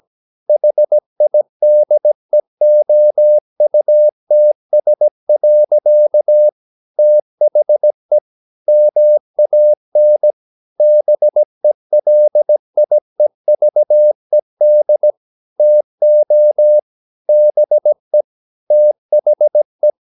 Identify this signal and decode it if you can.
Morse code, heard as a keyed tone, about 17 wpm, 600 Hz. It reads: HIDEOUTS. THE MAN BELIEVED TO BE THE
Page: Romanen Never af Ken Follet på engelsk.